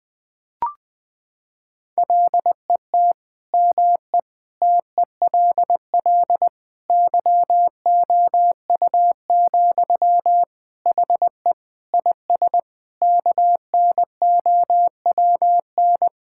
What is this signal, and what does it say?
Morse code, heard as a keyed tone, about 20 wpm, 700 Hz. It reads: LET ME TELL YOU, HE IS KNOWN
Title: Sentences from Top 1600 Words 20wpm